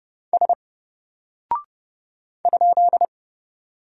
Morse code, heard as a keyed tone, about 30 wpm, 700 Hz. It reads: S ?
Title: Single Character (Letter:Number:Punctuation:Prosign) Alt 30wpm